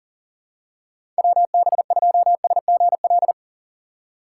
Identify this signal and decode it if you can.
Morse code, heard as a keyed tone, about 39 wpm, 700 Hz. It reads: WB2SGL
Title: USA Call Signs - 6 Characters - 2x Character Spacing - 40wpm